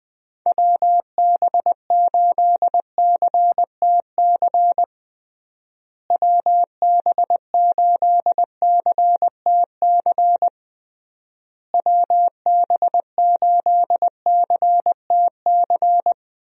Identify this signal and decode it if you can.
Morse code, heard as a keyed tone, about 20 wpm, 700 Hz. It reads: WB8CTC WB8CTC WB8CTC
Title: USA Call Signs - 6 Characters - Repeated 3x - 20wpm